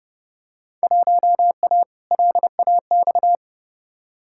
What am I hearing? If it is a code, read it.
Morse code, heard as a keyed tone, about 30 wpm, 700 Hz. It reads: 1A LAX
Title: Field Day Exchanges - 30wpm